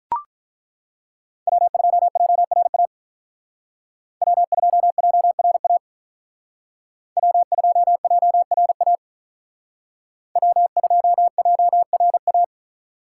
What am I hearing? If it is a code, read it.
Morse code, heard as a keyed tone, about 42 wpm, 700 Hz. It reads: W2JRA W2JRA W2JRA W2JRA
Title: Speed Racing - USA 1x3 Call Signs 35wpm